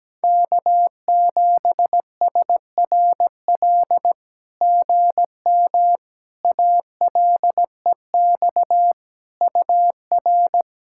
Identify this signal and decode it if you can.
Morse code, heard as a keyed tone, about 18 wpm, 700 Hz. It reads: K7SRL GM ALEX UR
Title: SOTA QSO Practice 17wpm